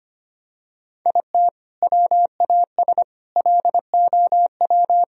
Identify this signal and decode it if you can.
Morse code, heard as a keyed tone, about 25 wpm, 700 Hz. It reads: IT WAS LOW
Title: Sentences from Top 1000 Words - Review Part 2 - 25wpm